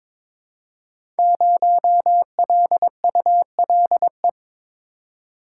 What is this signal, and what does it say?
Morse code, heard as a keyed tone, about 22 wpm, 700 Hz. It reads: 0LULE
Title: Code Groups - 5 Letter:Numbers - 22wpm